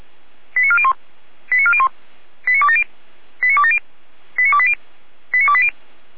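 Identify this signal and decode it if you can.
Five-tone selective call (ZVEI), heard as quick runs of five tones, tones of about 70 ms. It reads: 89401 89401 89270 89270 89280 89280
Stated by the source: ZVEI five-tone selective call 89701, received by SDR.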